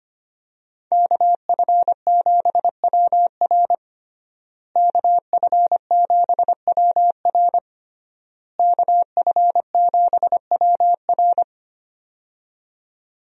Morse code, heard as a keyed tone, about 25 wpm, 700 Hz. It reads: KF7WR KF7WR KF7WR
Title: USA Call Signs - 5 Characters - Repeated 3x - 25wpm